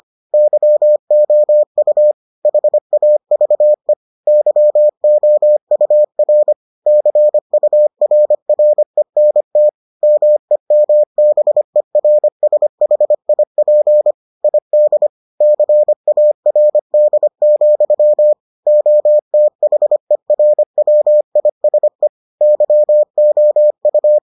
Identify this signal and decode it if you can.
Morse code, heard as a keyed tone, about 25 wpm, 600 Hz. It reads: YOU HAVE YOUR CURRENT MEMBERSHIP ID CARD, OTHERWISE YOU